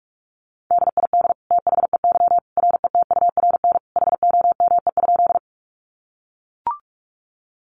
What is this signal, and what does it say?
Morse code, heard as a keyed tone, about 44 wpm, 700 Hz. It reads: DID THEY RETURN HOME?
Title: Sentences from Top 800 Words 45wpm